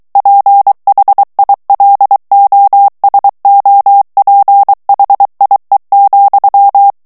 Morse code, heard as a keyed tone, about 23 wpm, 800 Hz. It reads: PHILOSOPHIE,
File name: morse mp3 / FRANCAIS TEXTE CLAIR 2 VITESSE 22 MOTS MINUTES